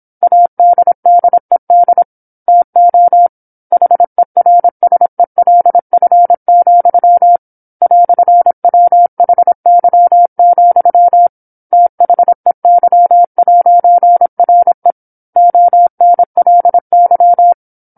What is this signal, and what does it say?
Morse code, heard as a keyed tone, about 27 wpm, 700 Hz. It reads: ADDED TO HERSELF, "WHY, THEY'RE ONLY
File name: 08-Chapter-08 26 WPM